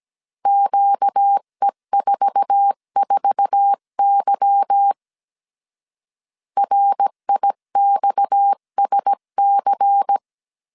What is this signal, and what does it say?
Morse code, heard as a keyed tone, about 17 wpm, 800 Hz. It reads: QE44Y RIXSC
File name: 1500C S B 16wpm